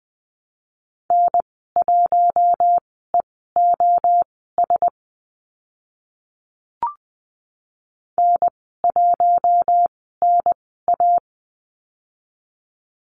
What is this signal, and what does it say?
Morse code, heard as a keyed tone, about 20 wpm, spaced as for 14 wpm, 700 Hz. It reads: N1EOS N1NA
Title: 042 Lesson 8g - Number 1 - Call Signs 20wpm